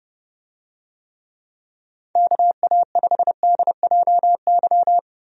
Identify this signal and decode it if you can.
Morse code, heard as a keyed tone, about 30 wpm, 700 Hz. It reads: KA5DJY
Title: USA Call Signs - 6 Characters - 30wpm